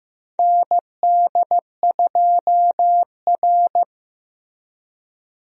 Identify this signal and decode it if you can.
Morse code, heard as a keyed tone, about 16 wpm, 700 Hz. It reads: ND2R
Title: USA Call Signs - 4 Characters - Repeated 3x - 15wpm